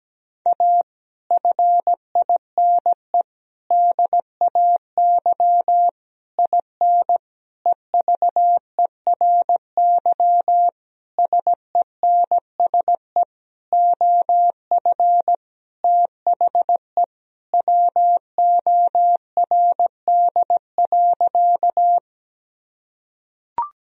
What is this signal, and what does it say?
Morse code, heard as a keyed tone, about 17 wpm, 700 Hz. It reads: A FINE DAY IN EVERY SENSE OF THE WORD.
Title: Sentences from Top 1000 Words - Review Part 4 - 17wpm